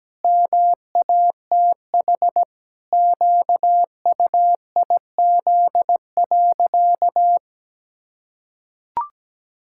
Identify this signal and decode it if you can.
Morse code, heard as a keyed tone, about 17 wpm, 700 Hz. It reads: MATH QUIZ.
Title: Pangram Challenge v2 - 17wpm